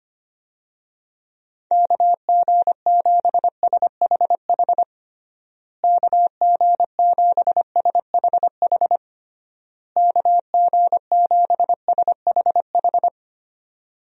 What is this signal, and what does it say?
Morse code, heard as a keyed tone, about 25 wpm, 700 Hz. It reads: KG7SHH KG7SHH KG7SHH
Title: USA Call Signs - 6 Characters - Repeated 3x - 25wpm